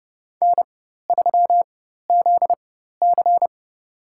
Morse code, heard as a keyed tone, about 30 wpm, spaced as for 14 wpm, 700 Hz. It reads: N3ZC
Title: USA Call Signs - 4 Characters - 4x Character Spacing - 30wpm